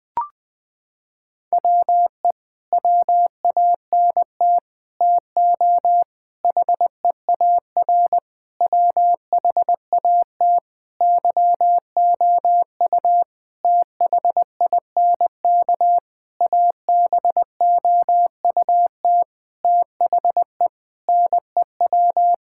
Morse code, heard as a keyed tone, about 20 wpm, 700 Hz. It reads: WE WANT TO HEAR WHAT YOU THINK ABOUT THE NEW
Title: Sentences with 8-Word Phrases - 20wpm